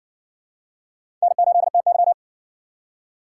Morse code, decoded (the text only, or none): N8TY